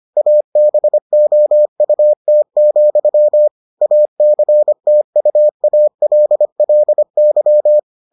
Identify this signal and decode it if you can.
Morse code, heard as a keyed tone, about 25 wpm, 600 Hz. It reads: ABOUT, ACTUALLY